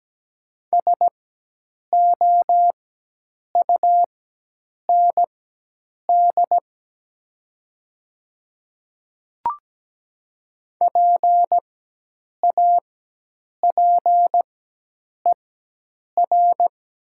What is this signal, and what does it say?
Morse code, heard as a keyed tone, about 17 wpm, spaced as for 8 wpm, 700 Hz. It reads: SOUND PAPER
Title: 4x Character Spacing - Top 300 Words - 17wpm